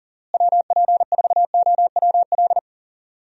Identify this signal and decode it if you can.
Morse code, heard as a keyed tone, about 40 wpm, 700 Hz. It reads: WP4OWL
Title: USA Call Signs - 6 Characters - Repeated 2x - 40wpm